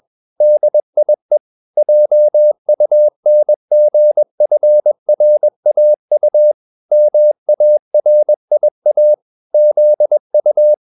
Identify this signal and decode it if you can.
Morse code, heard as a keyed tone, about 21 wpm, 600 Hz. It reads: DIE JUNGFRAU MARIA ZU
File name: marienkind2 21wpm